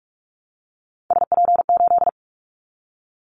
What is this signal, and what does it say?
Morse code, heard as a keyed tone, about 46 wpm, 700 Hz. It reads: SP8
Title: Callsign Prefixes for Top 25 DXCC - 45wpm